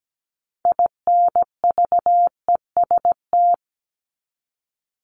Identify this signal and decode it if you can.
Morse code, heard as a keyed tone, about 17 wpm, 700 Hz. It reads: INVEST
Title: Top 1001-1250 Words 17wpm